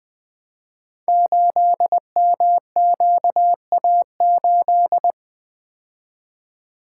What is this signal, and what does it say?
Morse code, heard as a keyed tone, about 20 wpm, 700 Hz. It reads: 8MQA8
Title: Code Groups - 5 Letter:Numbers - 20wpm